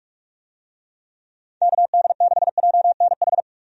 Morse code, heard as a keyed tone, about 46 wpm, 700 Hz. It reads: KD6JNH
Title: USA Call Signs - 6 Characters - 45wpm